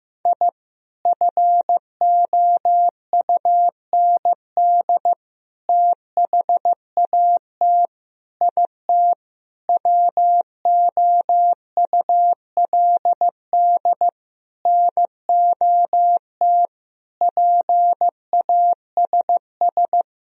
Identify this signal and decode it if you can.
Morse code, heard as a keyed tone, about 15 wpm, 700 Hz. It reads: I FOUND THAT IT WOULD NOT PASS
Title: Sentences from Top 1000 Words - Review Part 1 - 15wpm